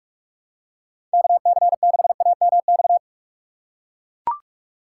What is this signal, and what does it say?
Morse code, heard as a keyed tone, about 41 wpm, 700 Hz. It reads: KC6AMX
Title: USA Call Signs - 6 Characters - 2x Character Spacing - 45wpm